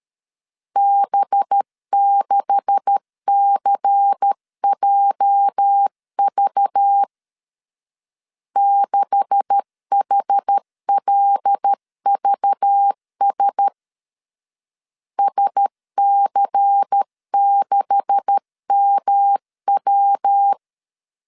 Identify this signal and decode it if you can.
Morse code, heard as a keyed tone, about 13 wpm, 800 Hz. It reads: B6CJV 6HLVS SC6MW
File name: LEZIONI / 9 286 12wpm 2305